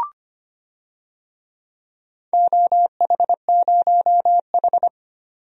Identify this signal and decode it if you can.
Morse code, heard as a keyed tone, about 25 wpm, 700 Hz. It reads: OH0H
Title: DXCC-100 Call Signs - 4 Characters - 25wpm